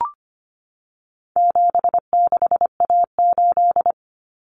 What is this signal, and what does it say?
Morse code, heard as a keyed tone, about 25 wpm, 700 Hz. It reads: 76A8
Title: Common Cut Numbers - 4-Digit 25wpm